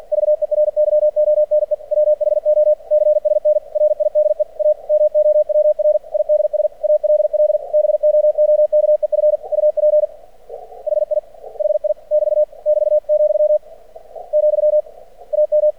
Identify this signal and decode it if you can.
Morse code, heard as a keyed tone, about 50 wpm, 600 Hz. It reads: VERYONE WHO CSN RIDE A MOWR IDS ALL BOOKEF UP HI HI <BT> <BT> , , AM
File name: archived / CRSnet